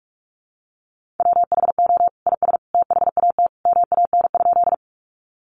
Morse code, heard as a keyed tone, about 44 wpm, 700 Hz. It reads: WHO IS THAT MAN?
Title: Sentences from Top 200 Words 45wpm